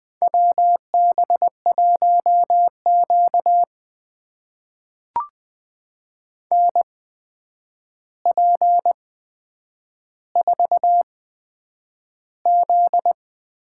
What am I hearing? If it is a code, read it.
Morse code, heard as a keyed tone, about 20 wpm, spaced as for 9 wpm, 700 Hz. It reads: WB1Q NP4Z
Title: USA Call Signs - 4 Characters - 8x Character Spacing - 20wpm